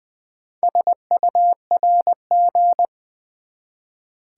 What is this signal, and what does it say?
Morse code, heard as a keyed tone, about 20 wpm, 700 Hz. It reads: SURG
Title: Build A Word - 1 Letter At A Time - 3 to 5-Letter Words - 20wpm